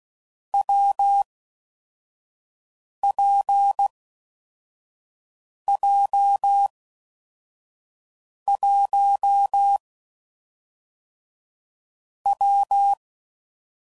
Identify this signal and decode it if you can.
Morse code, heard as a keyed tone, about 16 wpm, 800 Hz. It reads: W P J 1 W
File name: Combinations / Difficult Combos / WPJ WPJ1